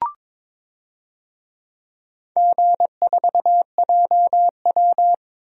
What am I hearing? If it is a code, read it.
Morse code, heard as a keyed tone, about 21 wpm, 700 Hz. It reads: G4JW
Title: DXCC-100 Call Signs - 4 Characters - 22wpm